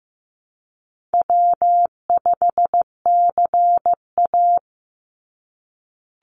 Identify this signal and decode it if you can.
Morse code, heard as a keyed tone, about 15 wpm, 700 Hz. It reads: W5CA
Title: USA Call Signs 15wpm